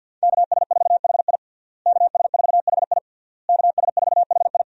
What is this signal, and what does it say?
Morse code, heard as a keyed tone, about 52 wpm, 700 Hz. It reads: KS4HI KS4HI KS4HI
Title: USA Call Signs - 5 Characters - Repeated 3x - 50wpm